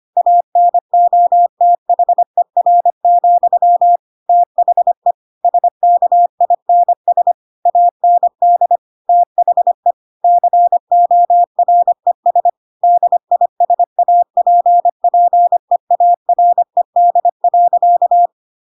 Morse code, heard as a keyed tone, about 25 wpm, 700 Hz. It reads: ANOTHER, THE SKINS AND THE CORES DISAPPEARED.
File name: Pinocchio / Pinocchio ch7 25 25 ik1hge